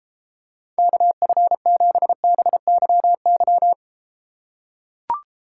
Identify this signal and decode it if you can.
Morse code, heard as a keyed tone, about 32 wpm, 700 Hz. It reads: KF7BYY